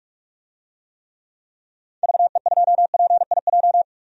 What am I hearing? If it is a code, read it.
Morse code, heard as a keyed tone, about 42 wpm, 700 Hz. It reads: VE2PIJ
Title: DXCC-100 Call Signs - 6 Characters - 45wpm